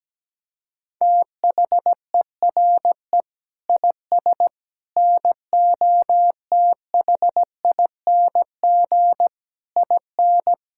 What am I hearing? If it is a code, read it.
Morse code, heard as a keyed tone, about 17 wpm, 700 Hz. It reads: THERE IS NOTHING IN
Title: Sentences from Top 1600 Words 17wpm